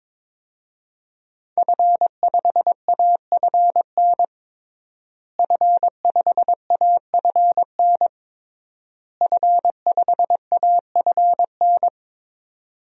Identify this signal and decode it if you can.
Morse code, heard as a keyed tone, about 22 wpm, 700 Hz. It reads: F5AFN F5AFN F5AFN
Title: DXCC-100 Call Signs - 5 Characters - Repeated 3x - 22wpm